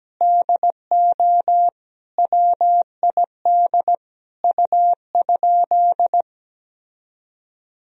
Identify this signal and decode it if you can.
Morse code, heard as a keyed tone, about 17 wpm, 700 Hz. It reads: DO WID U?
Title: Sentences from Top 100 Words and Abbreviations 17wpm